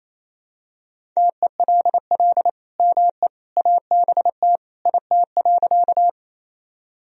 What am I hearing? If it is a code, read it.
Morse code, heard as a keyed tone, about 28 wpm, 700 Hz. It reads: TELL ME ABT IT.
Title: Sentences from Top 200 Words and Abbreviations 28wpm